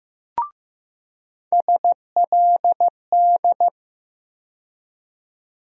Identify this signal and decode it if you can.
Morse code, heard as a keyed tone, about 15 wpm, 700 Hz. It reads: SLD
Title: USA Call Sign Postfixes - 3 Letters 15wpm